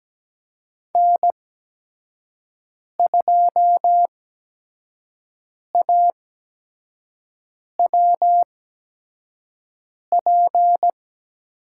Morse code, heard as a keyed tone, about 17 wpm, spaced as for 5 wpm, 700 Hz. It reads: N2AWP
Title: USA Call Signs - 8x Character Spacing - 17wpm